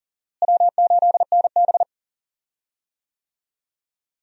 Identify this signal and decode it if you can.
Morse code, heard as a keyed tone, about 37 wpm, 700 Hz. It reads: W8NB
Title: USA Call Signs - 4 Characters - Repeated 2x - 40wpm